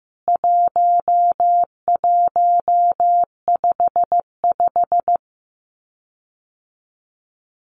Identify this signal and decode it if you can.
Morse code, heard as a keyed tone, about 14 wpm, 700 Hz. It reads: 1155
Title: CWT Exchange 15wpm